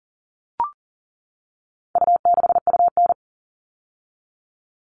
Morse code, heard as a keyed tone, about 40 wpm, 700 Hz. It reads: U6UN
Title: All Cut Numbers - 4-Digit 40wpm